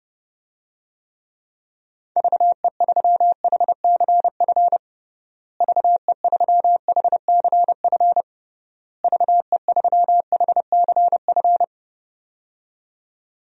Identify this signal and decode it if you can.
Morse code, heard as a keyed tone, about 30 wpm, 700 Hz. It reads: VE3HCF VE3HCF VE3HCF
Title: DXCC-100 Call Signs - 6 Characters - Repeated 3x - 30wpm